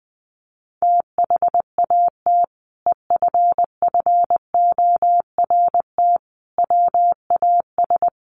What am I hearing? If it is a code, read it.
Morse code, heard as a keyed tone, about 20 wpm, 700 Hz. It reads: THAT EFFORT WAS
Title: Sentences from Top 1200 Words 20wpm